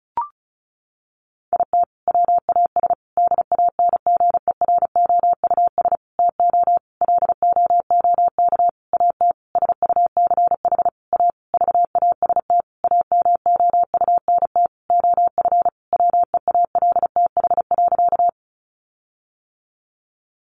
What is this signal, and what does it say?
Morse code, heard as a keyed tone, about 34 wpm, 700 Hz. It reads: IT WAS DANGEROUS TO LOOK AT SUCH A VAST AMOUNT OF WEALTH.
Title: Sentences from Top 1300 Words 35wpm